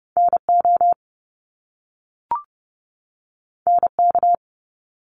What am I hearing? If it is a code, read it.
Morse code, heard as a keyed tone, about 30 wpm, 700 Hz. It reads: NO NK
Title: Top Two Letter Combinations in Words 30wpm